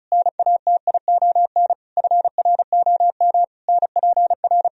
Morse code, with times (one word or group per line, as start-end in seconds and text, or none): 0.1-1.7 NATION
2.0-3.4 FROM
3.7-4.7 NPR